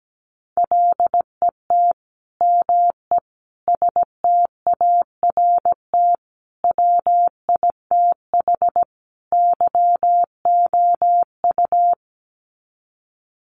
Morse code, heard as a keyed tone, about 17 wpm, 700 Hz. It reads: LET ME START WITH YOU
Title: Sets of 5 Words Encore 17wpm